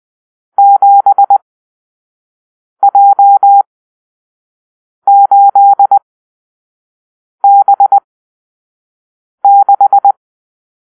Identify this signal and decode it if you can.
Morse code, heard as a keyed tone, about 20 wpm, spaced as for 5 wpm, 800 Hz. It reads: 7J8B6
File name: arrl / Grp-7 Drill-3